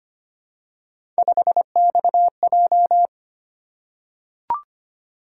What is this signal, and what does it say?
Morse code, heard as a keyed tone, about 25 wpm, 700 Hz. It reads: 5XJ
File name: Three Letter-Number Combinations - 25wpm